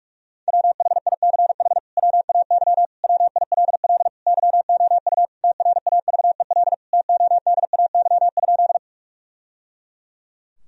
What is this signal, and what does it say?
Morse code, heard as a keyed tone, about 46 wpm, 700 Hz. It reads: WHICH WAY WILL YOU TRAVEL TODAY?
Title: Sentences from Top 1000 Words - Review Part 2 - 45wpm